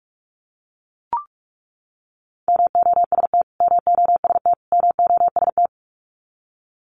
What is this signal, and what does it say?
Morse code, heard as a keyed tone, about 44 wpm, 700 Hz. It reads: MOST MOST MOST
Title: Top 300 Words Repeated 45wpm